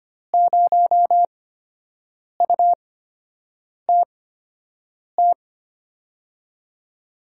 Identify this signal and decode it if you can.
Morse code, heard as a keyed tone, about 25 wpm, spaced as for 7 wpm, 700 Hz. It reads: 0UTT
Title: USA Call Signs - 8x Character Spacing - 25wpm